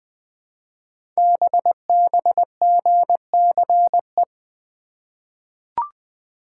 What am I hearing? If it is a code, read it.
Morse code, heard as a keyed tone, about 20 wpm, 700 Hz. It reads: BBGCE
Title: Code Groups - 5 Letter:Numbers - 20wpm